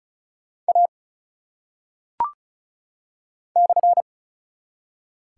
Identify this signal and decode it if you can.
Morse code, heard as a keyed tone, about 36 wpm, 700 Hz. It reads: A /
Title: Top 180 QSO Elements 35wpm